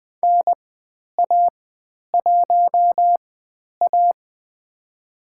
Morse code, heard as a keyed tone, about 20 wpm, spaced as for 10 wpm, 700 Hz. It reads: NA1A
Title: 102 Lesson 17g - Letter u - Call Signs 20wpm